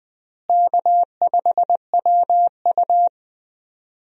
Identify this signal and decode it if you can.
Morse code, heard as a keyed tone, about 20 wpm, 700 Hz. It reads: K5WU
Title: USA Call Signs - 4 Characters - Repeated 3x - 20wpm